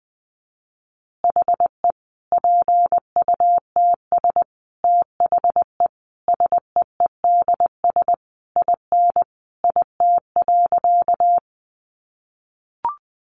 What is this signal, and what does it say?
Morse code, heard as a keyed tone, about 20 wpm, 700 Hz. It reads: HE PUTS THE SEEDS IN IT.
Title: Easy Phrases 20wpm